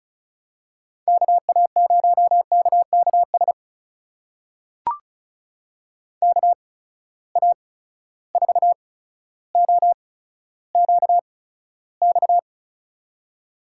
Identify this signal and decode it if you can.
Morse code, heard as a keyed tone, about 36 wpm, spaced as for 15 wpm, 700 Hz. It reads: KA0KKS KA4OQX
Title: USA Call Signs - 8x Character Spacing - 35wpm